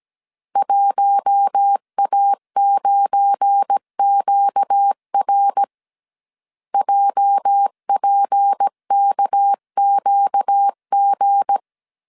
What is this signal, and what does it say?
Morse code, heard as a keyed tone, about 17 wpm, 800 Hz. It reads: 1A9QR JPKQG
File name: CW / DET / 1 9 1500C S A 16wpm 2828